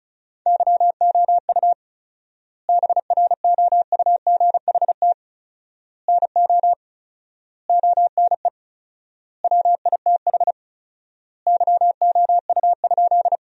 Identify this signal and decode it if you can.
Morse code, heard as a keyed tone, about 36 wpm, 700 Hz. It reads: YOU BROUGHT NO ONE WITH YOU?
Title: 4x Word-Spacing - Sentences from Top 500 Words - 35wpm